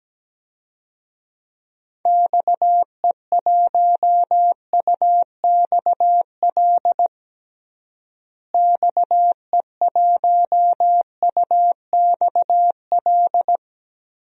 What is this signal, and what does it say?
Morse code, heard as a keyed tone, about 17 wpm, 700 Hz. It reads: XE1UXL XE1UXL
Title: DXCC-100 Call Signs - 6 Characters - Repeated 2x - 17wpm